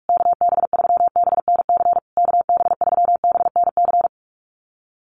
Morse code, heard as a keyed tone, about 44 wpm, 700 Hz. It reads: KB3BNC KB3BNC
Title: USA Call Signs Repeated 3x 45wpm